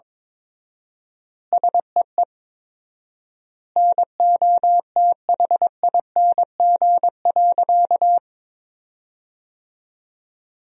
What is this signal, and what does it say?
Morse code, heard as a keyed tone, about 22 wpm, 700 Hz. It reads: SEE NOTHING.
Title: 4x Word-Spacing - Sentences from Top 500 Words - 22wpm